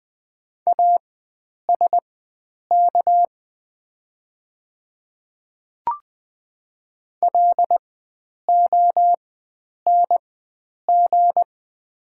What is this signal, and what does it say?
Morse code, heard as a keyed tone, about 20 wpm, spaced as for 9 wpm, 700 Hz. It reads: ASK LONG
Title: 4x Character Spacing - Top 300 Words - 20wpm